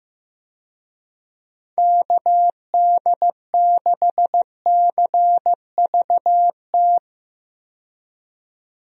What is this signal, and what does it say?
Morse code, heard as a keyed tone, about 15 wpm, 700 Hz. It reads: KD6CVT
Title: USA Call Signs - 6 Characters - 15wpm